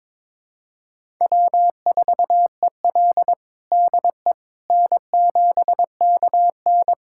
Morse code, heard as a keyed tone, about 21 wpm, 700 Hz. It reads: W4EL DE N7KN
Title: USA Callsign DE Callsign - 4 Characters - 22wpm